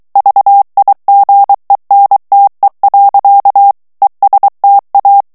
Morse code, heard as a keyed tone, about 23 wpm, 800 Hz. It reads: VIGENTE. ESTA
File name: morse mp3 / ESPAGNOL TEXTE EN CLAIR 2 VITESSE 22 MOTS MINUTES